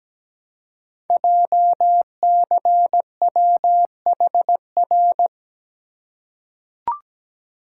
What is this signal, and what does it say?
Morse code, heard as a keyed tone, about 17 wpm, 700 Hz. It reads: JCWHR
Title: Code Groups - 5 Letters - 17wpm